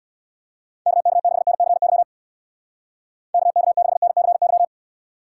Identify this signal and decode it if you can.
Morse code, heard as a keyed tone, about 65 wpm, 700 Hz. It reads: KK6NCY KK6NCY